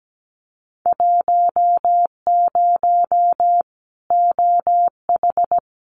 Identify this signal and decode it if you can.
Morse code, heard as a keyed tone, about 17 wpm, 700 Hz. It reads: 10 OH
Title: Winter Field Day Exchanges 17wpm